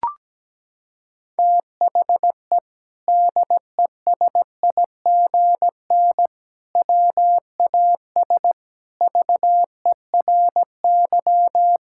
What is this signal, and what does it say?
Morse code, heard as a keyed tone, about 17 wpm, 700 Hz. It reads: THE DESIGN WAS VERY
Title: Sentences from Top 1000 Words - Review Part 5 - 17wpm